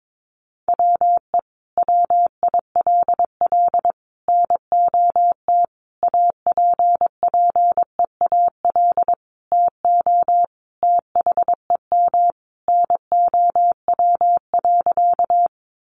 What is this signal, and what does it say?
Morse code, heard as a keyed tone, about 22 wpm, 700 Hz. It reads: WE WILL NOT APPEAL TO THEM NOW.
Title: Sentences from Top 1300 Words 22wpm